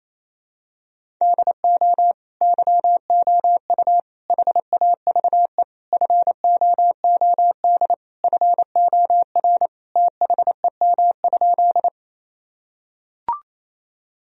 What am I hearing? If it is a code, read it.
Morse code, heard as a keyed tone, about 28 wpm, 700 Hz. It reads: DO YOU HAVE FOOD FOR THEM?
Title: Sentences from Top 1000 Words - Review Part 1 - 28wpm